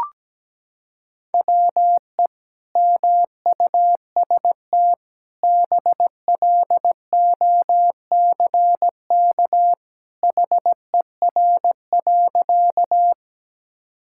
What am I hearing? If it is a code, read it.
Morse code, heard as a keyed tone, about 17 wpm, 700 Hz. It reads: WE MUST BLOCK HER.
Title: Sentences from Top 1600 Words 17wpm